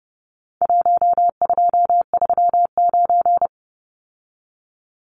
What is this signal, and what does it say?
Morse code, heard as a keyed tone, about 30 wpm, 700 Hz. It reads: 1239
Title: Numbers - 4-Digit 30wpm